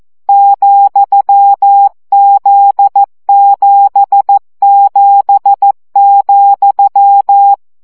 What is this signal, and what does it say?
Morse code, heard as a keyed tone, about 14 wpm, 800 Hz. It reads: ,Z77,
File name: CW gyakorlas z 7 vesszo 12 baud